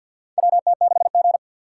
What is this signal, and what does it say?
Morse code, heard as a keyed tone, about 52 wpm, 700 Hz. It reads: WT6G
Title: USA Call Signs - 4 Characters - 3x Character Spacing - 50wpm